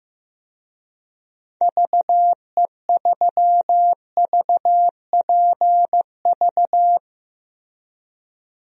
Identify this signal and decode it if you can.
Morse code, heard as a keyed tone, about 15 wpm, 700 Hz. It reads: VE3VPV
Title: DXCC-100 Call Signs - 6 Characters - Repeated 3x - 15wpm